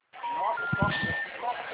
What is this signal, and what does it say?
Special information tone, with speech.